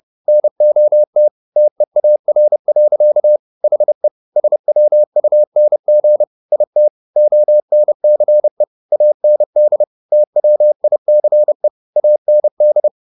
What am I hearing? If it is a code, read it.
Morse code, heard as a keyed tone, about 30 wpm, 600 Hz. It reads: NOT TEAR. HE SWUNG IT ONCE AND TWICE AND